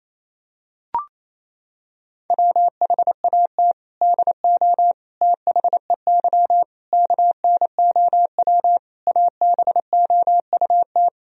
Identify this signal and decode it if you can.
Morse code, heard as a keyed tone, about 28 wpm, 700 Hz. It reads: WHAT DO THEY KNOW ABOUT
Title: Sentences from Top 1000 Words - Review Part 3 - 28wpm